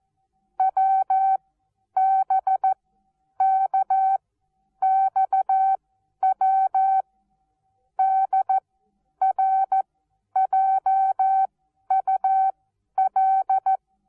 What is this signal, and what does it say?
Morse code, heard as a keyed tone, about 14 wpm, spaced as for 10 wpm, 770 Hz. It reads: WBKXW DRJUL
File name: Onlinelehrgang / morsen / DJ4UF Morsen19